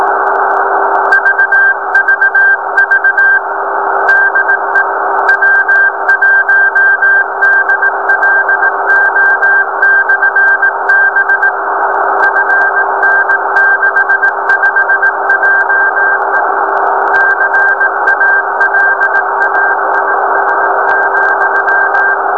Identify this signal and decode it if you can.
Morse code, heard as a keyed tone, about 18 wpm, 1600 Hz, with noise. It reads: VVV DE W1DLO/B FN65JE CALAIS,
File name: beacon / 2014 02-09 nx0p